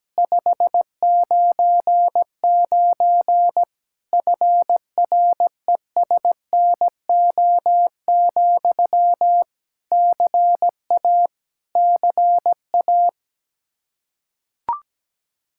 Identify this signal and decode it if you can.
Morse code, heard as a keyed tone, about 17 wpm, 700 Hz. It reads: 599 FRESNO, CA CA
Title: Practice QSO - Exchange 1 - Beginning - 17wpm